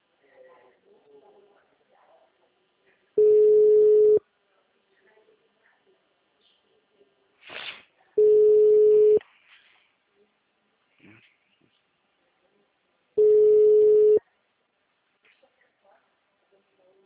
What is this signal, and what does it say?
Ringback tone (Europe, 425 Hz), with speech.